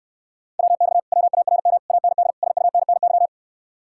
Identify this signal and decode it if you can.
Morse code, heard as a keyed tone, about 69 wpm, 700 Hz. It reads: WX WARM AND SUNNY